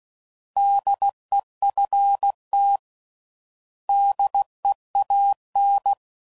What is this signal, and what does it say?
Morse code, heard as a keyed tone, about 16 wpm, 800 Hz. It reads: DEFT DEAN
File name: Prefixes / By First Letter / Ds